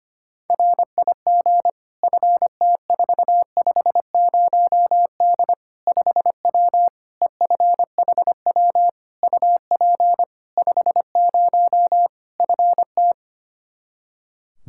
Morse code, heard as a keyed tone, about 25 wpm, 700 Hz. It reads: RIG FT450D 5W EFHW UP 50 FT
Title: Practice QSO - Exchange 2 - Info - 25wpm